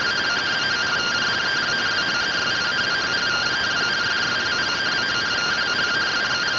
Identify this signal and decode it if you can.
Two-tone teleprinter signal, about 170 Hz shift.